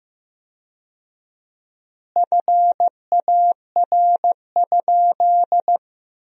Morse code, heard as a keyed tone, about 15 wpm, 700 Hz. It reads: FAR?
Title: 4x Word-Spacing - Sentences from Top 300 Words - 15wpm